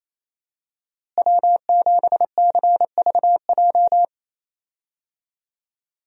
Morse code, heard as a keyed tone, about 28 wpm, 700 Hz. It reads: W7CVJ
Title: USA Call Signs - 5 Characters - Repeated 2x - 28wpm